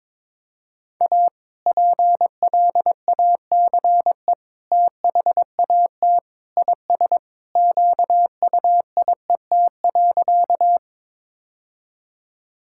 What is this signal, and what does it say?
Morse code, heard as a keyed tone, about 22 wpm, 700 Hz. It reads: A PLACE THAT IS QUIET.
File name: practice / 22wpm / Sentences with 2-Word Phrases - 22wpm